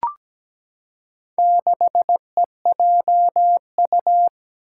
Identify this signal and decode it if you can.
Morse code, heard as a keyed tone, about 17 wpm, 700 Hz. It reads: 6EJU
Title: Four Challenging Letter-Number Combinations 17wpm